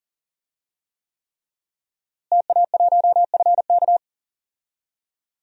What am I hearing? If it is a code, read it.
Morse code, heard as a keyed tone, about 39 wpm, 700 Hz. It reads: TA1FK
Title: DXCC-100 Call Signs - 5 Characters - Repeated 3x - 40wpm